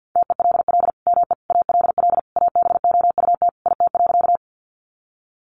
Morse code, heard as a keyed tone, about 50 wpm, 700 Hz. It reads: TELL ME ALL ABOUT IT.
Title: Sentences from Top 200 Words 50wpm